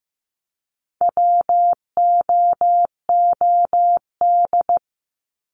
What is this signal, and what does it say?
Morse code, heard as a keyed tone, about 15 wpm, 700 Hz. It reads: WOOD
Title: Top 300 Words 15wpm